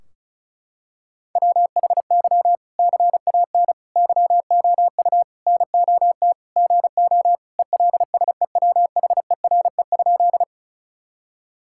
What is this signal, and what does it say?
Morse code, heard as a keyed tone, about 36 wpm, 700 Hz. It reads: WHY CAN YOU NOT GO ELSEWHERE?
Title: Sentences from Top 1600 Words 35wpm